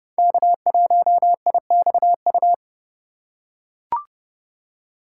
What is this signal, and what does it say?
Morse code, heard as a keyed tone, about 30 wpm, 700 Hz. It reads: K1IXU
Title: USA Call Signs - 5 Characters - 2x Character Spacing - 30wpm